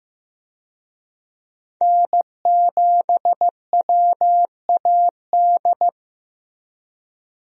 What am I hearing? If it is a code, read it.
Morse code, heard as a keyed tone, about 15 wpm, 700 Hz. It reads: N7WAD
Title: USA Call Signs - 5 Characters - Repeated 2x - 15wpm